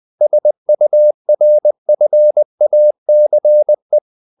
Morse code, heard as a keyed tone, about 20 wpm, 600 Hz. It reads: SURFACE
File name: the old man and the sea part 2 20wpm